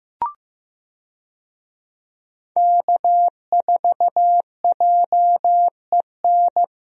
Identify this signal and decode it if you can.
Morse code, heard as a keyed tone, about 15 wpm, 700 Hz. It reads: K4JEN
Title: USA Call Signs - 5 Characters - Repeated 3x - 15wpm